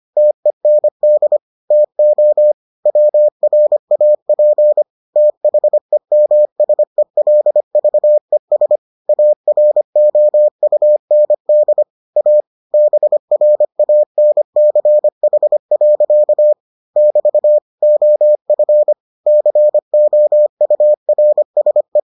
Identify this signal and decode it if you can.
Morse code, heard as a keyed tone, about 25 wpm, 600 Hz. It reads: TEND TO WRAP THEMSELVES AROUND A BRANCH. <BT> OF COURSE